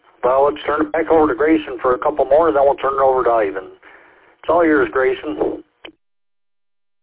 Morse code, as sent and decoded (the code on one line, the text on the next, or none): none